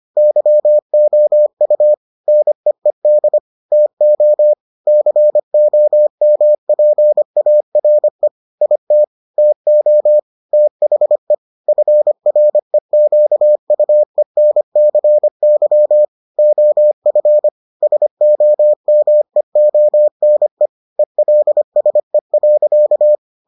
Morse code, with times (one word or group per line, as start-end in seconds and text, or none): 0.2-1.9 YOU
2.3-3.4 NEED
3.7-4.5 TO
4.9-8.3 COMPARE
8.6-9.0 IT
9.4-10.2 TO
10.5-11.3 THE
11.7-16.1 FREQUENCY
16.4-17.5 OF
17.8-20.7 SOMEONE
21.0-23.2 ELSE.